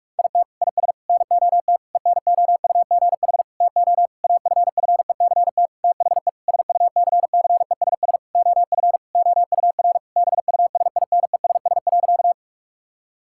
Morse code, heard as a keyed tone, about 46 wpm, 700 Hz. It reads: IT IS NOT ENOUGH TO AFFECT THE SUCCESS OF OUR BUSINESS.
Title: Sentences from Top 1000 Words - Review Part 6 - 45wpm